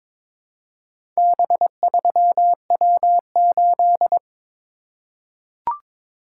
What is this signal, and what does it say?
Morse code, heard as a keyed tone, about 23 wpm, 700 Hz. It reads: B3W8
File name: practice / 22wpm / Four Letter-Number Combinations - 22wpm